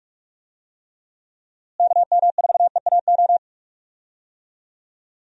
Morse code, heard as a keyed tone, about 44 wpm, 700 Hz. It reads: KM4EAO
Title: USA Call Signs - 6 Characters - Repeated 3x - 45wpm